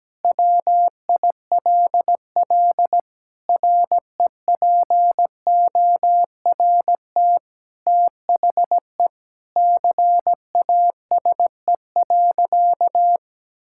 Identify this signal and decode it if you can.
Morse code, heard as a keyed tone, about 17 wpm, 700 Hz. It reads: WILL REPORT THE CASE.
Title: Sentences from Top 1000 Words - Review Part 6 - 17wpm